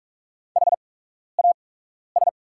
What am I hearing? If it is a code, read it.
Morse code, heard as a keyed tone, about 44 wpm, spaced as for 12 wpm, 700 Hz. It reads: HAS
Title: 8x Character Spacing - Top 100 Words - 45wpm